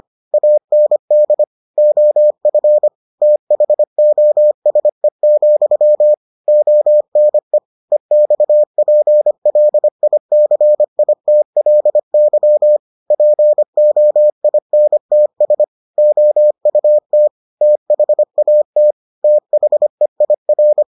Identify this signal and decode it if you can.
Morse code, heard as a keyed tone, about 25 wpm, 600 Hz. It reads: AND OF THOSE, ONE EXPLICITLY POINTS OUT THAT THEIR